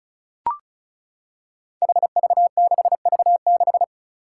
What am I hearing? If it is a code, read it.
Morse code, heard as a keyed tone, about 34 wpm, 700 Hz. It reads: HV6V6
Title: Five Challenging Letter-Number Combinations 35wpm